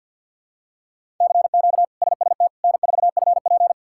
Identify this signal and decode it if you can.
Morse code, heard as a keyed tone, about 50 wpm, 700 Hz. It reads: CQ SST N4FP
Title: K1USN SST QSO Practice - 50wpm